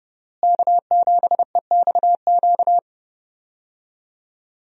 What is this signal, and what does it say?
Morse code, heard as a keyed tone, about 29 wpm, 700 Hz. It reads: K7EXQ
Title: USA Call Signs - 5 Characters - Repeated 3x - 30wpm